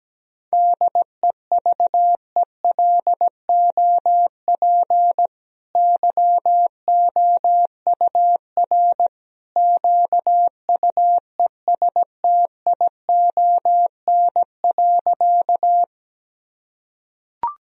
Morse code, heard as a keyed tone, about 17 wpm, 700 Hz. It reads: DEVELOP YOUR QUESTION.
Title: Sentences from Top 1000 Words - Review Part 2 - 17wpm